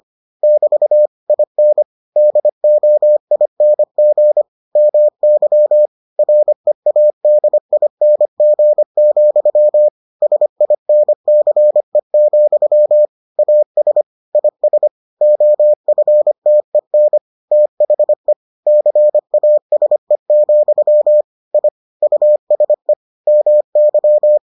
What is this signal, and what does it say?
Morse code, heard as a keyed tone, about 25 wpm, 600 Hz. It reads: <BT> IN DOING MY READING, SINCE, AS IS OFTEN THE CASE, I USE MY